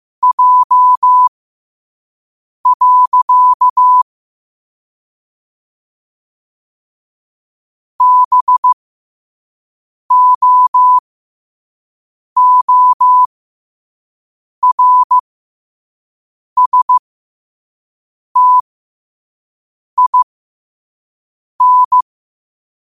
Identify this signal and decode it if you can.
Morse code, heard as a keyed tone, about 15 wpm, spaced as for 5 wpm, 1000 Hz. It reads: J. BOORSTIN
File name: qotdfiles / 5wpm